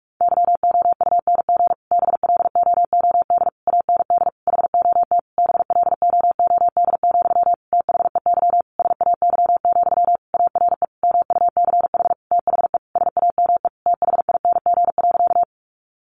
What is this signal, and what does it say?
Morse code, heard as a keyed tone, about 44 wpm, 700 Hz. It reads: YOUNG BLOOD AND HOT BLOOD, THEY SAY, ARE MUCH THE SAME THING.
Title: Sentences from Top 1100 Words 45wpm